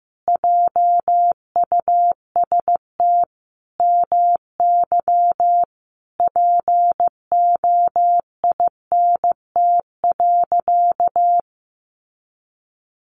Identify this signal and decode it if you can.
Morse code, heard as a keyed tone, about 15 wpm, 700 Hz. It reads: JUST MY POINT.
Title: Sentences from Top 200 Words 15wpm